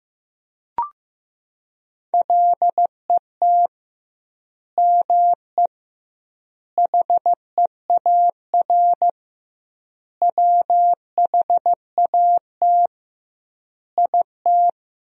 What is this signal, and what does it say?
Morse code, heard as a keyed tone, about 15 wpm, 700 Hz. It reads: LET ME HEAR WHAT IT
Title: 2x Word-Spacing - Sentences from Top 300 Words - 15wpm